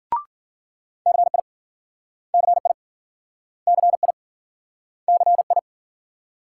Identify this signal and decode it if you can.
Morse code, heard as a keyed tone, about 52 wpm, 700 Hz. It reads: CI CI CI CI